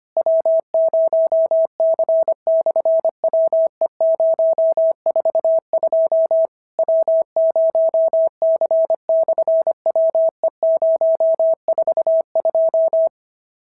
Morse code, heard as a keyed tone, about 24 wpm, 650 Hz. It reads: W0C/WE042 W0C/WE042
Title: SOTA QSO Practice 25wpm